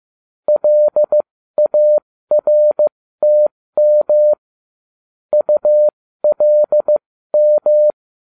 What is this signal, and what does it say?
Morse code, heard as a keyed tone, about 15 wpm, spaced as for 13 wpm, 600 Hz. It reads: LARTM ULM
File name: cw / ubacw09 1